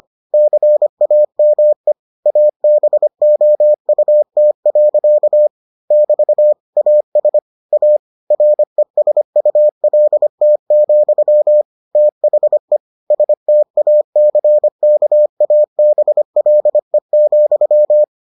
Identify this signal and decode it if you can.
Morse code, heard as a keyed tone, about 25 wpm, 600 Hz. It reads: CAME ABOUT. <BT> AS A RESULT, THE STACKABLE,